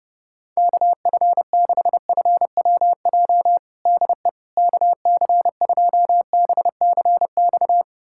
Morse code, heard as a keyed tone, about 29 wpm, 700 Hz. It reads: KF6FWJ DE KC2BCX
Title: USA Callsign DE Callsign - 6 Characters - 30wpm